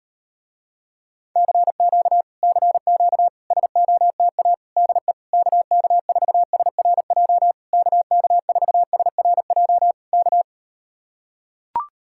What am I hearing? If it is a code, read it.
Morse code, heard as a keyed tone, about 38 wpm, 700 Hz. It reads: CQ CQ SOTA DE KK4SRJ KK4SRJ K